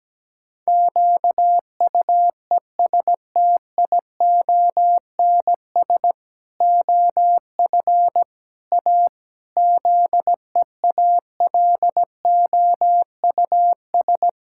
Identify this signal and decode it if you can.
Morse code, heard as a keyed tone, about 17 wpm, 700 Hz. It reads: QUESTIONS OF A ZEALOUS
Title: Pangram Challenge v2 - 17wpm